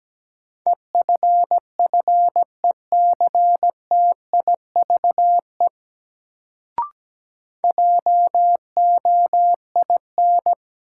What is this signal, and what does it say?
Morse code, heard as a keyed tone, about 17 wpm, 700 Hz. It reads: EFFECTIVE JOIN
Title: Sentences from Top 1000 Words - Review Part 6 - New Word Drill - 17wpm